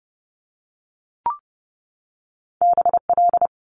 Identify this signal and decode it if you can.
Morse code, heard as a keyed tone, about 30 wpm, 700 Hz. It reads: BL
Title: Multi-letter Phonemes 30wpm